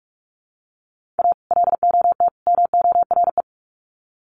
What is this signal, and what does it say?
Morse code, heard as a keyed tone, about 44 wpm, 700 Hz. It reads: A LOT MORE
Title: Sets of 3 Words 45wpm